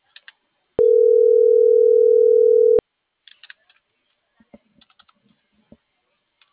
Ringback tone (North America), 440 + 480 Hz.